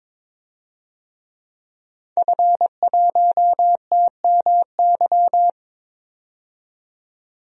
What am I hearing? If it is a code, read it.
Morse code, heard as a keyed tone, about 21 wpm, 700 Hz. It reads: F1TMY
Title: DXCC-100 Call Signs - 5 Characters - 22wpm